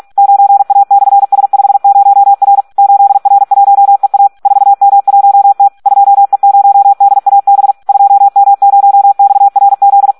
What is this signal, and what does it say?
Morse code, heard as a keyed tone, about 46 wpm, 800 Hz. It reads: 9A/S50R 8R1EA 4M1T 2E0DAB 2M0XRZ
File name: Audio practica CW / SD 1 46